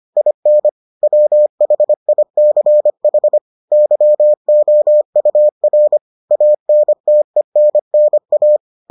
Morse code, heard as a keyed tone, about 25 wpm, 600 Hz. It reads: IN WHICH YOUR ANTENNA